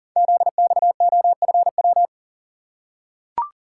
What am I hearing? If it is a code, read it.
Morse code, heard as a keyed tone, about 39 wpm, 690 Hz. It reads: ZXOFW
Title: Code Groups - 5 Letter:Numbers - 40wpm